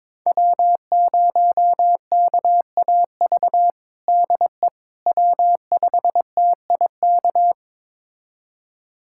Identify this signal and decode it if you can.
Morse code, heard as a keyed tone, about 22 wpm, 700 Hz. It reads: W0KAV DE W5TIK
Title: USA Callsign DE Callsign - 5 Characters - 22wpm